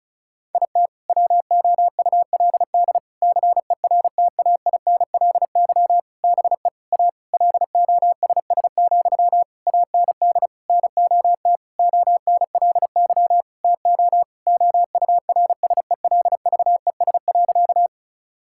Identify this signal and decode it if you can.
Morse code, heard as a keyed tone, about 34 wpm, 700 Hz. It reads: IT WOULD CERTAINLY BE A LOSS, AND NOT ONLY TO OURSELVES.
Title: Sentences from Top 1600 Words 35wpm